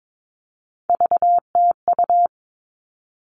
Morse code, heard as a keyed tone, about 22 wpm, 700 Hz. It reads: VTU
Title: All Cut Numbers - 3-Digit 22wpm